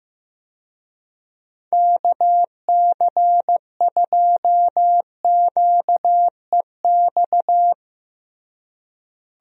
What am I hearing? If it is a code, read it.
Morse code, heard as a keyed tone, about 15 wpm, 700 Hz. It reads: KC2QEX
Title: USA Call Signs - 6 Characters - Repeated 2x - 15wpm